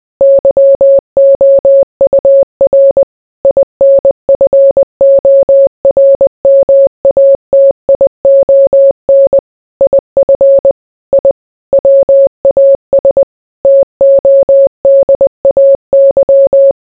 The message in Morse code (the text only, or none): YOUR INFORMATION IF I WAS TODAY